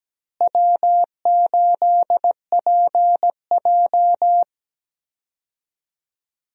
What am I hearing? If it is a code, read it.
Morse code, heard as a keyed tone, about 17 wpm, 700 Hz. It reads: W8PJ
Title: USA Call Signs - 4 Characters - Repeated 3x - 17wpm